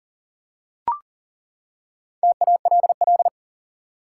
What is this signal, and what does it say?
Morse code, heard as a keyed tone, about 40 wpm, 700 Hz. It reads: TALL
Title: Four Letter Words - 40wpm